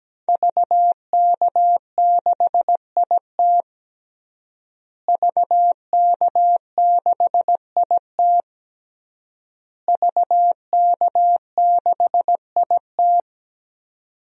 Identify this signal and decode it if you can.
Morse code, heard as a keyed tone, about 17 wpm, 700 Hz. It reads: VK6IT VK6IT VK6IT
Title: DXCC-100 Call Signs - 5 Characters - Repeated 3x - 17wpm